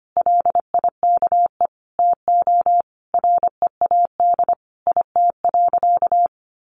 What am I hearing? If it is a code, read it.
Morse code, heard as a keyed tone, about 25 wpm, 700 Hz. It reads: LIKE TO READ IT.
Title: Sentences from Top 500 Words 25wpm